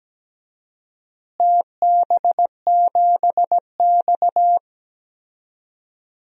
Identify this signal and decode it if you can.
Morse code, heard as a keyed tone, about 17 wpm, 700 Hz. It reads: TB7X
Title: DXCC-100 Call Signs - 4 Characters - Repeated 2x - 17wpm